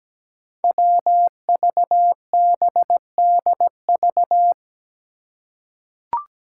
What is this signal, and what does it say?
Morse code, heard as a keyed tone, about 17 wpm, 700 Hz. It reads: WVBDV
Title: Five Challenging Letter-Number Combinations 17wpm